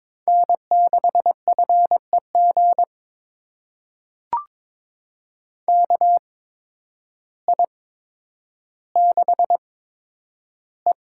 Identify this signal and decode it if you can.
Morse code, heard as a keyed tone, about 22 wpm, spaced as for 10 wpm, 700 Hz. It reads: N6FEG KI6E
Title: USA Call Signs - 5 Characters - 8x Character Spacing - 22wpm